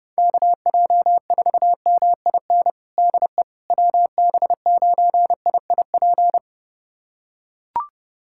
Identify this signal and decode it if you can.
Morse code, heard as a keyed tone, about 30 wpm, 700 Hz. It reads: KJ4MIN DE WB9IIP
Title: USA Callsign DE Callsign - 6 Characters - 30wpm